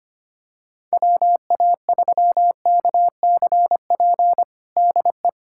Morse code, heard as a keyed tone, about 25 wpm, 700 Hz. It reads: WA3KCP DE
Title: USA Callsign DE Callsign - 6 Characters - 25wpm